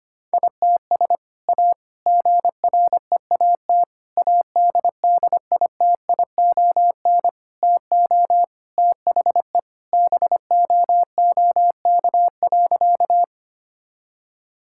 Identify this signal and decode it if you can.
Morse code, heard as a keyed tone, about 25 wpm, 700 Hz. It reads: ITS A GREAT ADDITION TO THE BOOK.
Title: Sentences from Top 1000 Words - Review Part 6 - 25wpm